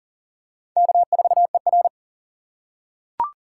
Morse code, heard as a keyed tone, about 40 wpm, 700 Hz. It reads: K4ER
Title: USA Call Signs - 4 Characters - 3x Character Spacing - 40wpm